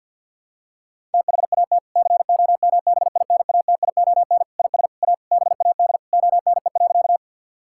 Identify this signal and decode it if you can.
Morse code, heard as a keyed tone, about 50 wpm, 700 Hz. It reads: THAT COMBINATION IS A BAD ONE.
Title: Sentences from Top 1000 Words - Review Part 6 - 50wpm